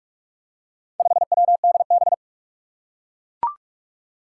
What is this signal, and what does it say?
Morse code, heard as a keyed tone, about 44 wpm, 700 Hz. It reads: 5WDB
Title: Four Challenging Letter-Number Combinations 45wpm